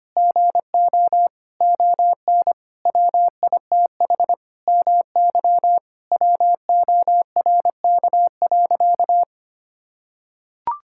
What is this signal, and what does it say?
Morse code, heard as a keyed tone, about 25 wpm, 700 Hz. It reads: GO ON WITH MY WORK.
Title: Sentences from Top 1000 Words - Review Part 6 - 25wpm